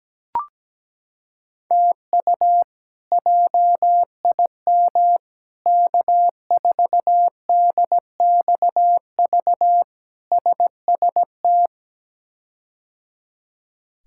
Morse code, heard as a keyed tone, about 17 wpm, 700 Hz. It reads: TU JIM K4DXV SST
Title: K1USN SST QSO Practice - 17wpm